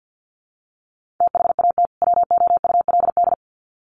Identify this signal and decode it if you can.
Morse code, heard as a keyed tone, about 50 wpm, 700 Hz. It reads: THAT WOULD
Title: Sets of 2 Words 50wpm